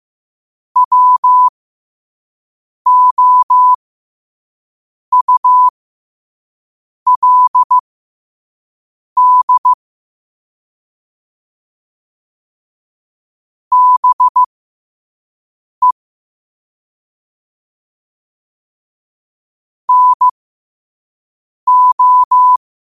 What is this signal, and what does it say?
Morse code, heard as a keyed tone, about 15 wpm, spaced as for 5 wpm, 1000 Hz. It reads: WOULD BE NO
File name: qotdfiles / 5wpm